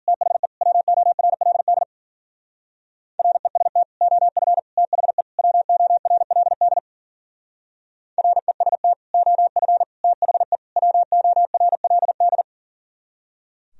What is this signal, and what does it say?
Morse code, heard as a keyed tone, about 52 wpm, 700 Hz. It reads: THE WORLD REST OF THE WORLD REST OF THE WORLD